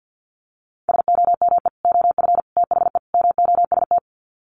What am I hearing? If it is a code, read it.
Morse code, heard as a keyed tone, about 50 wpm, 700 Hz. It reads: SOME OF THE MOST
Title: Sets of 4 Words 50wpm